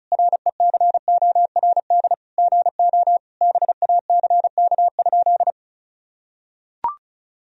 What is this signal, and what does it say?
Morse code, heard as a keyed tone, about 36 wpm, 700 Hz. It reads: RECORD GO BACK?
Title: Sentences from Top 1000 Words - Review Part 4 - 35wpm